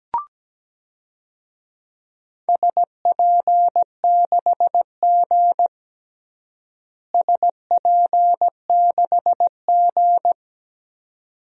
Morse code, heard as a keyed tone, about 17 wpm, 700 Hz. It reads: SP6G SP6G
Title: DXCC-100 Call Signs - 4 Characters - Repeated 2x - 17wpm